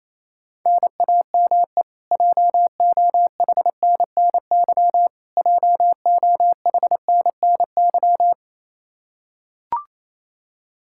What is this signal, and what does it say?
Morse code, heard as a keyed tone, about 28 wpm, 700 Hz. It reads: NAME JOHNNY JOHNNY
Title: Practice QSO - Exchange 1 - Beginning - 28wpm